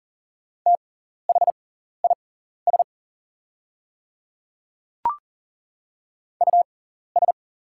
Morse code, heard as a keyed tone, about 40 wpm, spaced as for 14 wpm, 690 Hz. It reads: THIS US
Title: 6x Character Spacing - Top 200 Words - 40wpm